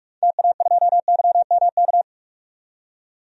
Morse code, heard as a keyed tone, about 44 wpm, 700 Hz. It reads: TA2YMK
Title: DXCC-100 Call Signs - 6 Characters - 45wpm